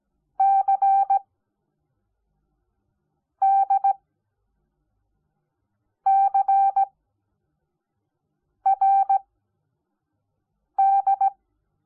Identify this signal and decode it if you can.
Morse code, heard as a keyed tone, about 17 wpm, spaced as for 4 wpm, 780 Hz. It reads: CDCRD
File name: Onlinelehrgang / morsen / DJ4UF Morsen05